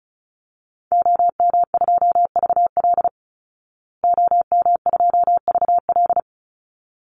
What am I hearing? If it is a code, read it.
Morse code, heard as a keyed tone, about 34 wpm, 700 Hz. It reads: OM2VL OM2VL
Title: Call Signs Repeated 3x - 35wpm